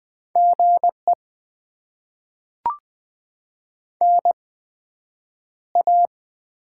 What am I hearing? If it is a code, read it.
Morse code, heard as a keyed tone, about 20 wpm, spaced as for 9 wpm, 700 Hz. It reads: GE NA
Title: Top 20 QSO Elements - 8x Character Spacing - 20wpm